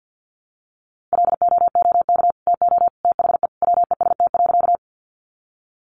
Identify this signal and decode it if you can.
Morse code, heard as a keyed tone, about 50 wpm, 700 Hz. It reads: LOOK TO THE WEST.
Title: Sentences from Top 400 Words 50wpm